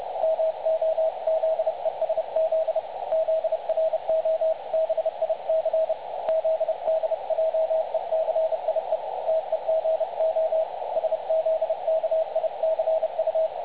Morse code, heard as a keyed tone, about 29 wpm, 660 Hz. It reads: MOZESZ ZROBIC ZDJECIE TEGO SZKICU
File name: dlazaawansowanych / nagraniazpasma / websdr recording 3565.0kHz